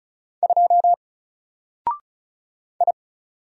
Morse code, heard as a keyed tone, about 35 wpm, 700 Hz. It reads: 2 I
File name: Single-Letter-Number-35wpm